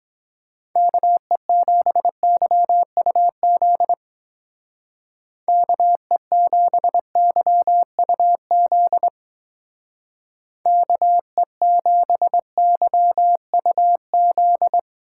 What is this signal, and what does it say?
Morse code, heard as a keyed tone, about 21 wpm, 700 Hz. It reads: KE7YUZ KE7YUZ KE7YUZ